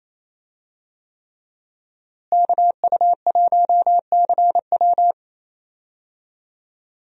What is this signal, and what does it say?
Morse code, heard as a keyed tone, about 28 wpm, 700 Hz. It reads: KU1CW
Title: USA Call Signs - 5 Characters - Repeated 3x - 28wpm